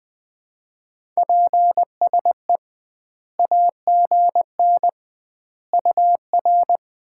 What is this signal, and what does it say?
Morse code, heard as a keyed tone, about 20 wpm, 700 Hz. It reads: PSE AGN UR
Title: Common QSO Phrases - 2x Word-Spacing - 20wpm